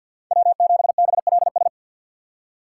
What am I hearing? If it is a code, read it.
Morse code, heard as a keyed tone, about 52 wpm, 700 Hz. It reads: W7BLS
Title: USA Call Signs - 5 Characters - 2x Character Spacing - 50wpm